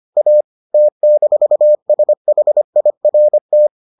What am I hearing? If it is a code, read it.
Morse code, heard as a keyed tone, about 25 wpm, 600 Hz. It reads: A T-SHIRT